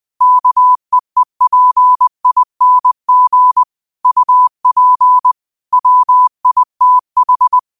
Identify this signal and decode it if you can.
Morse code, heard as a keyed tone, about 20 wpm, 1000 Hz. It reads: KEEPING UP WITH